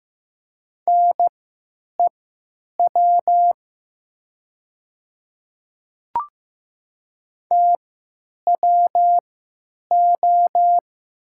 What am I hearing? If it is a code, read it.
Morse code, heard as a keyed tone, about 15 wpm, spaced as for 9 wpm, 700 Hz. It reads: NEW TWO
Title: 3x Character Spacing - Top 200 Words - 15wpm